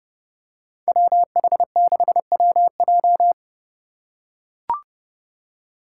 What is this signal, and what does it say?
Morse code, heard as a keyed tone, about 30 wpm, 700 Hz. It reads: WH6WJ
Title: USA Call Signs - 5 Characters - 3x Character Spacing - 30wpm